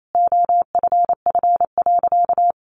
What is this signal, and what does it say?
Morse code, heard as a keyed tone, about 28 wpm, 700 Hz. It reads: OFF.
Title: Sentences from Top 1000 Words 28wpm